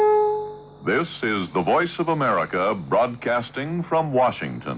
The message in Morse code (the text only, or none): none